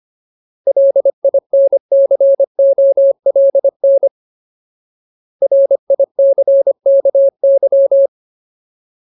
Morse code, heard as a keyed tone, about 25 wpm, 550 Hz. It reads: LINCOLN RICKY
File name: Ultimate Word List / 25WPM Ultimate